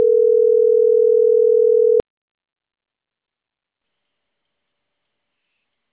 Ringback tone (North America), 440 + 480 Hz.